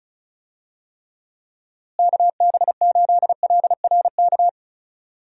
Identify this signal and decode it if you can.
Morse code, heard as a keyed tone, about 36 wpm, 700 Hz. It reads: KB8LRK
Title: USA Call Signs - 6 Characters - Repeated 3x - 35wpm